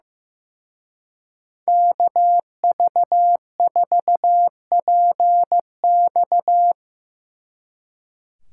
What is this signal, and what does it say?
Morse code, heard as a keyed tone, about 17 wpm, 700 Hz. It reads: KV4PX
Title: Speed Racing - USA 2x2 Call Signs 15wpm